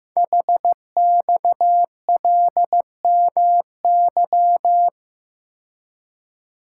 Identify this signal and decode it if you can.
Morse code, heard as a keyed tone, about 15 wpm, 700 Hz. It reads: HXLMY
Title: Code Groups - 5 Letters - 15wpm